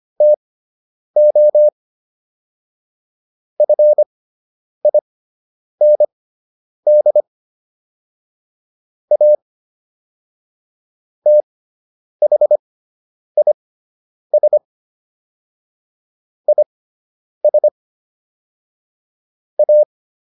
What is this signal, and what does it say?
Morse code, heard as a keyed tone, about 25 wpm, spaced as for 9 wpm, 600 Hz. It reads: TO FIND A THIS IS A